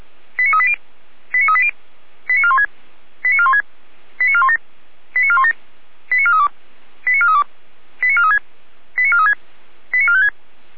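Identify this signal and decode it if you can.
Five-tone selective call (ZVEI), heard as quick runs of five tones, tones of about 70 ms. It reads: 89380 89380 89416 89416 89417 89417 89432 89432 89436 89436 89456
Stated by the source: ZVEI five-tone selective call received by SDR.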